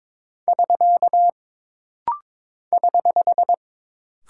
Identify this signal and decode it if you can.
Morse code, heard as a keyed tone, about 22 wpm, 700 Hz. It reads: <SK> <HH>
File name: practice / 22wpm / Single Punctuation-Prosign - Rapid-Fire - 22wpm